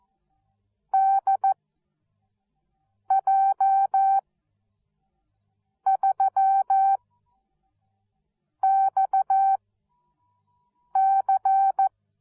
Morse code, heard as a keyed tone, about 14 wpm, spaced as for 5 wpm, 780 Hz. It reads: DJ3XC
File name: Onlinelehrgang / morsen / DJ4UF Morsen13